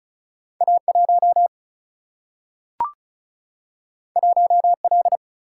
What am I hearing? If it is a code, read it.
Morse code, heard as a keyed tone, about 36 wpm, 700 Hz. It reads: A 1 1 L
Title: Single-Letter Single-Number 35wpm